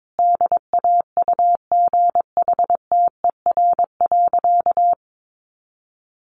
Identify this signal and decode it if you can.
Morse code, heard as a keyed tone, about 22 wpm, 700 Hz. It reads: DAUGHTER.
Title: Sentences from Top 1500 Words 22wpm